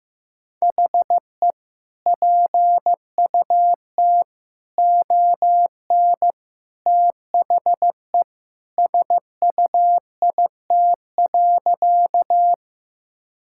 Morse code, heard as a keyed tone, about 15 wpm, 700 Hz. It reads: HE PUT ON THE SUIT.
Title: Sentences from Top 1600 Words 15wpm